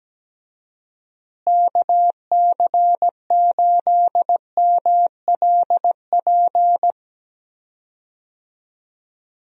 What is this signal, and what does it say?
Morse code, heard as a keyed tone, about 17 wpm, 700 Hz. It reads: KC8MLP
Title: USA Call Signs - 6 Characters - Repeated 3x - 17wpm